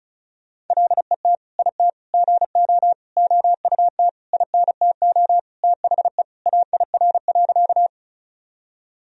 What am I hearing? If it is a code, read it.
Morse code, heard as a keyed tone, about 34 wpm, 700 Hz. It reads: LET IT GO OUT INTO THE AIR.
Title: Sentences from Top 1000 Words - Review Part 1 - 35wpm